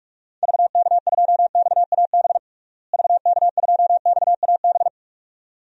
Morse code, heard as a keyed tone, about 46 wpm, 700 Hz. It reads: VK2XAB VK2XAB
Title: DXCC-100 Call Signs - 6 Characters - Repeated 2x - 45wpm